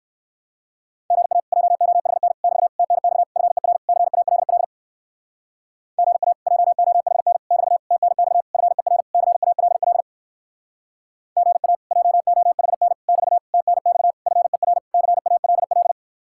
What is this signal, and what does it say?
Morse code, heard as a keyed tone, about 59 wpm, 700 Hz. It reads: GA JOHN <BT> TNX FER CALL GA JOHN <BT> TNX FER CALL GA JOHN <BT> TNX FER CALL